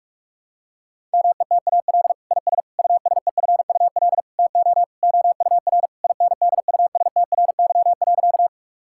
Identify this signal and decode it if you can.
Morse code, heard as a keyed tone, about 41 wpm, 700 Hz. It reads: METAL IS USEFUL TO OUR INDUSTRY.
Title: Sentences from Top 1000 Words - Review Part 6 - 45wpm